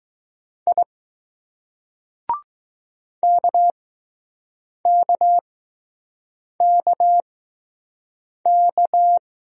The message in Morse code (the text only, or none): I KKKK